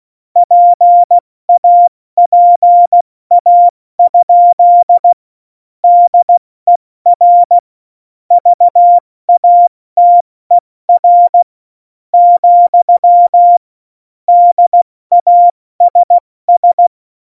Morse code, heard as a keyed tone, about 16 wpm, spaced as for 13 wpm, 700 Hz. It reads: PAPA? DER VATER, DASS
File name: Morsekurs / Witze / Witz 11. Juni 2019, 70 F80